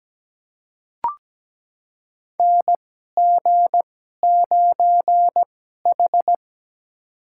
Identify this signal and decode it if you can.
Morse code, heard as a keyed tone, about 17 wpm, spaced as for 12 wpm, 700 Hz. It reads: NG9H
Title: USA Call Signs - 4 Characters - 2x Character Spacing - 17wpm